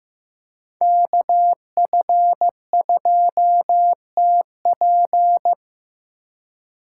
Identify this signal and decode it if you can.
Morse code, heard as a keyed tone, about 15 wpm, 700 Hz. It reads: KF2TP
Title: K1USN SST QSO Practice - 15wpm